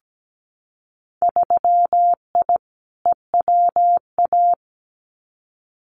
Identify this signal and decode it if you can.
Morse code, heard as a keyed tone, about 17 wpm, 700 Hz. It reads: 3I EWA
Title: Winter Field Day Exchanges Encore 17wpm